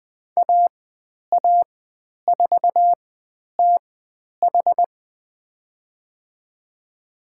Morse code, heard as a keyed tone, about 20 wpm, spaced as for 10 wpm, 700 Hz. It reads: AA4TH
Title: 095 Lesson 16g - Letter c - Call Signs 20wpm